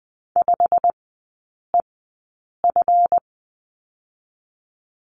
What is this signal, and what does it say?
Morse code, heard as a keyed tone, about 20 wpm, spaced as for 8 wpm, 700 Hz. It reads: 5EF
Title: 145 Lesson 24e - Letter y - Three Characters 20wpm